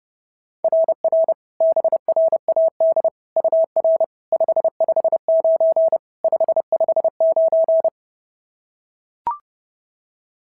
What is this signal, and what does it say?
Morse code, heard as a keyed tone, about 26 wpm, 650 Hz. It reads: RR BRAD UR 559 559
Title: SOTA QSO Practice 30wpm